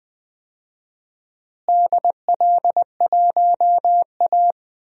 Morse code, heard as a keyed tone, about 20 wpm, 700 Hz. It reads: DL1A
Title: DXCC-100 Call Signs - 4 Characters - Repeated 3x - 20wpm